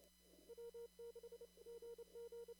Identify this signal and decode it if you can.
Morse code, heard as a keyed tone, about 28 wpm, 470 Hz. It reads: W6GG
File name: CW / kt5j wpx cw-op-k5pi